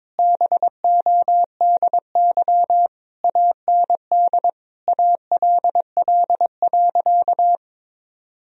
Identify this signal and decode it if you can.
Morse code, heard as a keyed tone, about 22 wpm, 700 Hz. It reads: BODY AND ALL.
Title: Sentences from Top 1000 Words - Review Part 2 - 22wpm